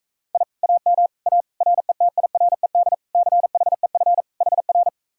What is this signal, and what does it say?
Morse code, heard as a keyed tone, about 45 wpm, 700 Hz. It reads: I AM A RETIRED CHEF HR